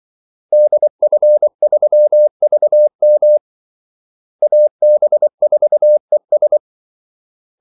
Morse code, heard as a keyed tone, about 24 wpm, 600 Hz. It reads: DF3VM AB4ES
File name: Callsigns-24wpm